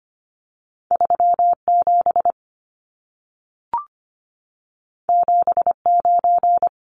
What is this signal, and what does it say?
Morse code, heard as a keyed tone, about 25 wpm, 700 Hz. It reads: 37 79
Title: Numbers - 2-Digit 25wpm